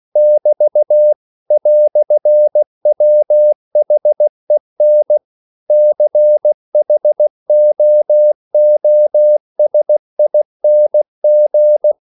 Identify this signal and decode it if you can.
Morse code, heard as a keyed tone, about 16 wpm, 600 Hz. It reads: <BT> "WHEN CHOOSING